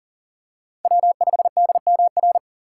Morse code, heard as a keyed tone, about 40 wpm, 700 Hz. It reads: W5DMR
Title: USA Call Signs - 5 Characters - 2x Character Spacing - 40wpm